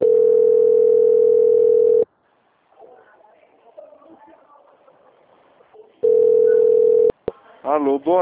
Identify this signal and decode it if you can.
Ringback tone (North America), 440 + 480 Hz, with speech.
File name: out- - - - - .23559